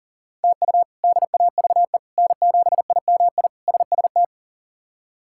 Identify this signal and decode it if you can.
Morse code, heard as a keyed tone, about 40 wpm, 690 Hz. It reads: TU DAVE N7IMI SST
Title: K1USN SST QSO Practice - 40wpm